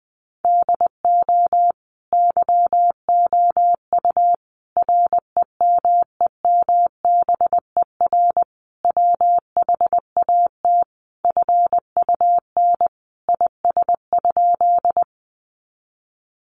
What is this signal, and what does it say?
Morse code, heard as a keyed tone, about 20 wpm, 700 Hz. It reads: DO YOU REMEMBER WHAT FUN IS?
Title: Sentences from Top 700 Words 20wpm